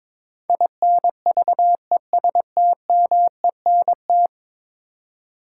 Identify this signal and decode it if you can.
Morse code, heard as a keyed tone, about 21 wpm, 700 Hz. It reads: INVESTMENT
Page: morse-code-ninja-practice-22wpm